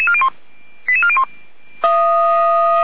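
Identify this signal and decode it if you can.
Five-tone selective call (ZVEI), heard as quick runs of five tones, tones of about 70 ms. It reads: 88401
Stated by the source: Received on an SDR.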